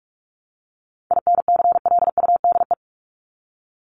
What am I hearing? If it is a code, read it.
Morse code, heard as a keyed tone, about 44 wpm, 700 Hz. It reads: INCLUDE
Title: ED Suffix - Top 100 Words with Base Words 45wpm